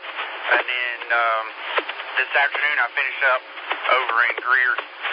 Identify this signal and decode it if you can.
Emergency Alert System (SAME) tones (header bursts), heard as harsh buzzing bursts, with speech. It is heard from 0.4 to 4.8 s.